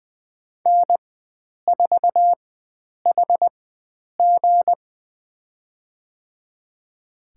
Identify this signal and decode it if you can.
Morse code, heard as a keyed tone, about 20 wpm, spaced as for 9 wpm, 700 Hz. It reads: N4HG
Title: USA Call Signs - 4 Characters - 4x Character Spacing - 20wpm